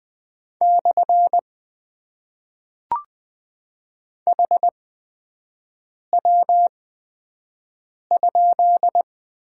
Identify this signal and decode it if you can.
Morse code, heard as a keyed tone, about 20 wpm, spaced as for 5 wpm, 700 Hz. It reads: / HW?
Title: Top 90 QSO Elements - 8x Character Spacing - 20wpm